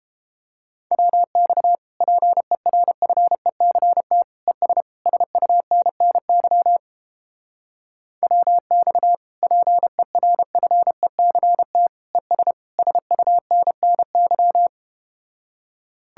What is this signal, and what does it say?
Morse code, heard as a keyed tone, about 32 wpm, 700 Hz. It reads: WX PERFECT ES SUNNY WX PERFECT ES SUNNY